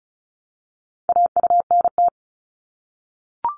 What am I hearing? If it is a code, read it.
Morse code, heard as a keyed tone, about 36 wpm, 700 Hz. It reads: AUNT
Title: Multi-letter Phonemes Repeated 3x 35wpm